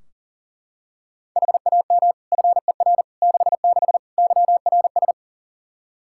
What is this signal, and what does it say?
Morse code, heard as a keyed tone, about 40 wpm, 690 Hz. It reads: HAM FER 66 YRS
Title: Practice QSO - Exchange 3 - More Info - 40wpm